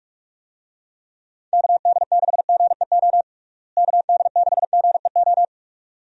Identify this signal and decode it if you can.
Morse code, heard as a keyed tone, about 47 wpm, 700 Hz. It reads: KD6GEO KD6GEO
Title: USA Call Signs - 6 Characters - Repeated 2x - 45wpm